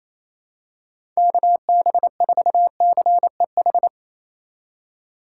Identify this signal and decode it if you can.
Morse code, heard as a keyed tone, about 28 wpm, 700 Hz. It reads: KB4CEH
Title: USA Call Signs - 6 Characters - 3x Character Spacing - 28wpm